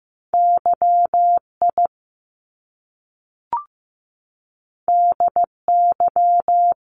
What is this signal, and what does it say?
Morse code, heard as a keyed tone, about 15 wpm, 700 Hz. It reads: YI DY
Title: Top Two Letter Combinations in Words 15wpm